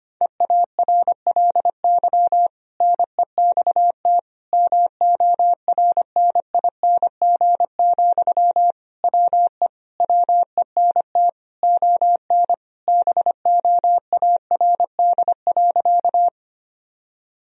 Morse code, heard as a keyed tone, about 25 wpm, 700 Hz. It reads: EARLY NEXT MORNING, WE WENT ON BOARD.
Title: Sentences from Top 1000 Words - Review Part 4 - 25wpm